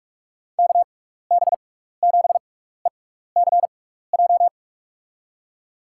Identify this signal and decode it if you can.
Morse code, heard as a keyed tone, about 44 wpm, spaced as for 15 wpm, 700 Hz. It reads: KB7ECJ
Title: USA Call Signs - 6 Characters - 6x Character Spacing - 45wpm